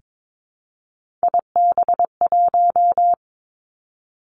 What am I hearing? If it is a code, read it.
Morse code, heard as a keyed tone, about 22 wpm, 700 Hz. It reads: IB1
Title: Three Challenging Letter-Number Combinations 22wpm